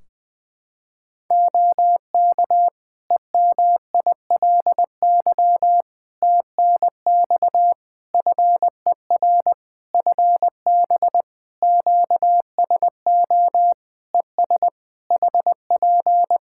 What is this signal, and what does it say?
Morse code, heard as a keyed tone, about 20 wpm, 700 Hz. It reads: OK EMILY TNX FER FB QSO ES HP
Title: Practice QSO - Exchange 4 - Ending - 20wpm